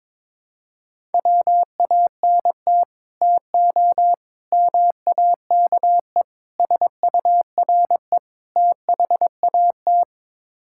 Morse code, heard as a keyed tone, about 22 wpm, 700 Hz. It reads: WANT TO MAKE SURE THAT
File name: practice / 22wpm / Build A Phrase - 5-Word Phrases - 22wpm